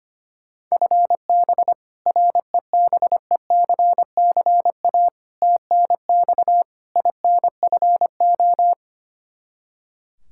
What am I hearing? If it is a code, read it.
Morse code, heard as a keyed tone, about 25 wpm, 700 Hz. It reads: FB REBECCA TNX INFO
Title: Practice QSO - Exchange 3 - More Info - 25wpm